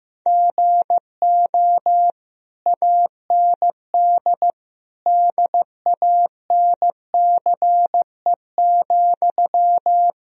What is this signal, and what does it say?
Morse code, heard as a keyed tone, about 15 wpm, 700 Hz. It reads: GO AND DANCE,
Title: Sentences from Top 1000 Words - Review Part 4 - 15wpm